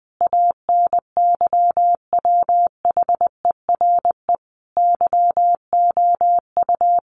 Morse code, heard as a keyed tone, about 20 wpm, 700 Hz. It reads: ANYWHERE YOU
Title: Sentences from Top 1400 Words 20wpm